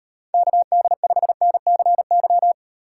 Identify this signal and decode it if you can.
Morse code, heard as a keyed tone, about 38 wpm, 700 Hz. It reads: KD5NCY